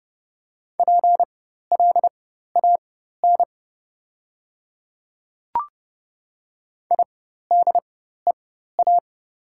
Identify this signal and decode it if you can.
Morse code, heard as a keyed tone, about 30 wpm, spaced as for 14 wpm, 700 Hz. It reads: PLAN IDEA
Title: 4x Character Spacing - Top 500 Words - 30wpm